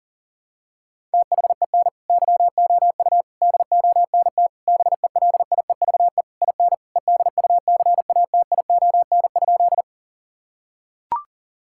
Morse code, heard as a keyed tone, about 37 wpm, 700 Hz. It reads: THEN YOU DONT BELIEVE IN EDUCATION?
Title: Sentences from Top 1000 Words - Review Part 6 - 40wpm